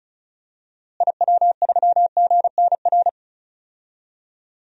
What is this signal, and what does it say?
Morse code, heard as a keyed tone, about 36 wpm, 700 Hz. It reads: IW3GNR
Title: DXCC-100 Call Signs - 6 Characters - Repeated 2x - 35wpm